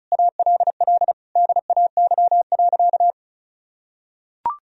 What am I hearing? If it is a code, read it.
Morse code, heard as a keyed tone, about 36 wpm, 700 Hz. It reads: ALL DAY.
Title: Sentences from Top 1000 Words - Review Part 2 - 35wpm